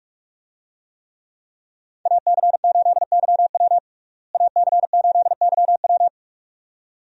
Morse code, heard as a keyed tone, about 44 wpm, 700 Hz. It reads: AC8YW AC8YW
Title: USA Call Signs - 5 Characters - Repeated 2x - 45wpm